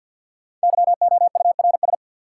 Morse code, heard as a keyed tone, about 49 wpm, 700 Hz. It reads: YOURS
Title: Sentences from Top 1500 Words - New Word Drill - 50wpm